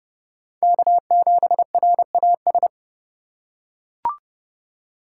Morse code, heard as a keyed tone, about 30 wpm, 700 Hz. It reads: K7RAS
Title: USA Call Signs - 5 Characters - 4x Character Spacing - 30wpm